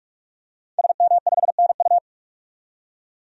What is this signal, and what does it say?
Morse code, heard as a keyed tone, about 46 wpm, 700 Hz. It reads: SM5NU
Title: DXCC-100 Call Signs - 5 Characters - 45wpm